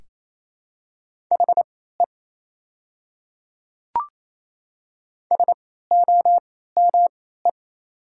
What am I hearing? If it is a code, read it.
Morse code, heard as a keyed tone, about 28 wpm, spaced as for 16 wpm, 700 Hz. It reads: HE SOME
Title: 3x Character Spacing - Top 100 Words - 28wpm